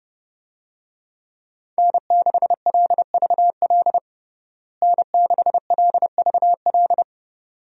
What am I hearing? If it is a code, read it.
Morse code, heard as a keyed tone, about 30 wpm, 700 Hz. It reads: N6LVL N6LVL
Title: USA Call Signs - 5 Characters - Repeated 2x - 30wpm